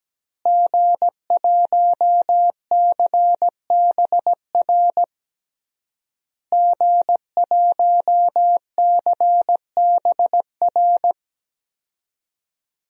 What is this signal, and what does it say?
Morse code, heard as a keyed tone, about 17 wpm, 700 Hz. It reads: G1CBR G1CBR
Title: DXCC-100 Call Signs - 5 Characters - Repeated 3x - 17wpm